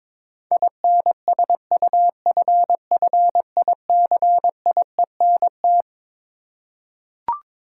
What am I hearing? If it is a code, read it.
Morse code, heard as a keyed tone, about 22 wpm, 700 Hz. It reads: INSUFFICIENT
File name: Twelve Letter Words - 22wpm